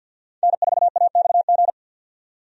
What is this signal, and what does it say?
Morse code, heard as a keyed tone, about 49 wpm, 700 Hz. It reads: N4AXG
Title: USA Call Signs - 5 Characters - 50wpm